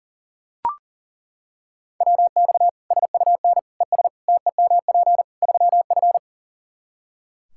Morse code, heard as a keyed tone, about 40 wpm, 700 Hz. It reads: WX SUN ES TEMP 3F
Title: Practice QSO - Exchange 2 - Info - 40wpm